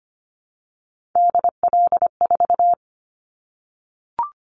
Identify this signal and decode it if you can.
Morse code, heard as a keyed tone, about 25 wpm, 700 Hz. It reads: DL4
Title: Callsign Prefixes for Top 25 DXCC - 25wpm